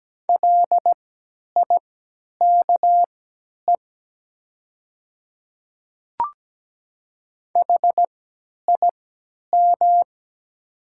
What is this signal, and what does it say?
Morse code, heard as a keyed tone, about 17 wpm, spaced as for 10 wpm, 700 Hz. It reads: LIKE HIM
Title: 3x Character Spacing - Top 100 Words - 17wpm